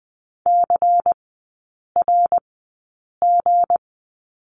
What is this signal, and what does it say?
Morse code, heard as a keyed tone, about 20 wpm, spaced as for 8 wpm, 700 Hz. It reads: CRG
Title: 159 Lesson 26e - Letter g - Three Characters 20wpm